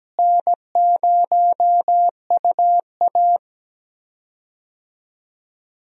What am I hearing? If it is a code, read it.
Morse code, heard as a keyed tone, about 17 wpm, 700 Hz. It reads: N0UA
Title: USA Call Signs - 4 Characters - Repeated 3x - 17wpm